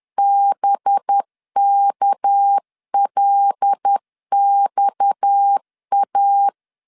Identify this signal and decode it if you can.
Morse code, heard as a keyed tone, about 10 wpm, 800 Hz. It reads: BKLXA
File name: LEZIONI / 6 PKX0 / 6 PKX 10wpm 2426